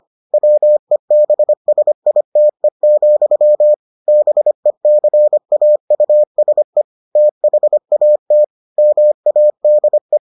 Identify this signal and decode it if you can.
Morse code, heard as a keyed tone, about 25 wpm, 600 Hz. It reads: WEBSITE, BECAUSE THAT MADE